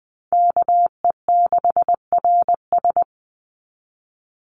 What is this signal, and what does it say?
Morse code, heard as a keyed tone, about 20 wpm, 700 Hz. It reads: KE6RS
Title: NA SOTA Chaser Callsigns 20wpm